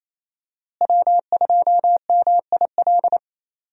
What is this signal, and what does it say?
Morse code, heard as a keyed tone, about 28 wpm, 700 Hz. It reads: W2MIL
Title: USA Call Signs - 5 Characters - Repeated 2x - 28wpm